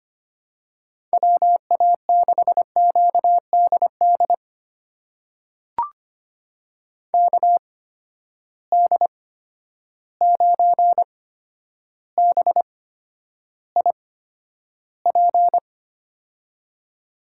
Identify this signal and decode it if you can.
Morse code, heard as a keyed tone, about 25 wpm, spaced as for 11 wpm, 700 Hz. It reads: WA6QDD KD9BIP
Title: USA Call Signs - 6 Characters - 8x Character Spacing - 25wpm